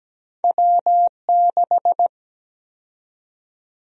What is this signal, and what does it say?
Morse code, heard as a keyed tone, about 17 wpm, 700 Hz. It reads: W6
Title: USA Call Sign Prefixes - Repeated 2x - 17wpm